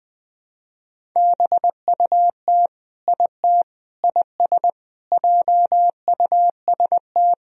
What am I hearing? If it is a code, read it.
Morse code, heard as a keyed tone, about 20 wpm, 700 Hz. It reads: BUT IT IS JUST
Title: Sentences from Top 200 Words and Abbreviations 20wpm